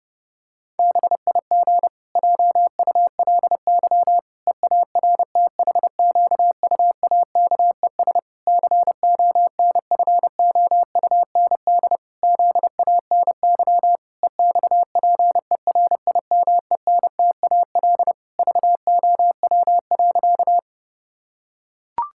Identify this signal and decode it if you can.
Morse code, heard as a keyed tone, about 32 wpm, 700 Hz. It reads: BIG JULY EARTHQUAKES CONFOUND ZANY EXPERIMENTAL VOW.
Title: Pangram Challenge v2 - 30wpm